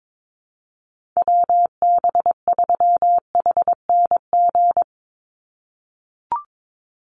Morse code, heard as a keyed tone, about 22 wpm, 700 Hz. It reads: WB3HNG
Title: USA Call Signs 22wpm